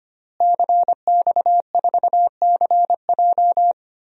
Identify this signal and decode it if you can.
Morse code, heard as a keyed tone, about 25 wpm, 700 Hz. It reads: CX4CJ
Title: DXCC-100 Call Signs - 5 Characters - Repeated 3x - 25wpm